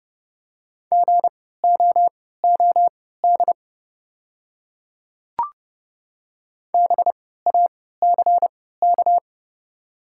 Morse code, heard as a keyed tone, about 30 wpm, spaced as for 17 wpm, 690 Hz. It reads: GOOD BACK
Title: 3x Character Spacing - Top 200 Words - 30wpm